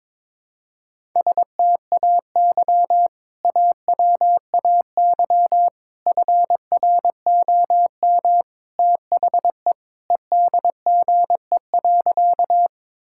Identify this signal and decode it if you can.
Morse code, heard as a keyed tone, about 22 wpm, 700 Hz. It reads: STAY AWAY FROM THE EDGE.
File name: practice / 22wpm / Sentences with 3-Word Phrases - 22wpm